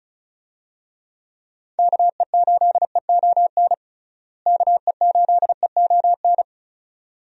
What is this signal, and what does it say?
Morse code, heard as a keyed tone, about 36 wpm, 700 Hz. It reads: KE8EON KE8EON
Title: USA Call Signs - 6 Characters - Repeated 2x - 35wpm